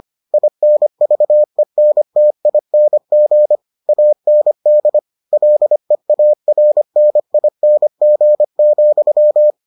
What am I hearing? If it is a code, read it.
Morse code, heard as a keyed tone, about 25 wpm, 600 Hz. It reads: INVENTING AND LEARNING,